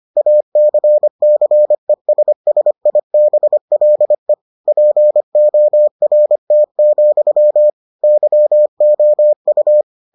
Morse code, heard as a keyed tone, about 25 wpm, 600 Hz. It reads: ACCESSIBLE PORT, YOU